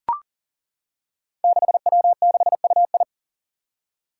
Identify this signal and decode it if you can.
Morse code, heard as a keyed tone, about 38 wpm, 700 Hz. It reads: 6W6UI
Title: Five Challenging Letter-Number Combinations 40wpm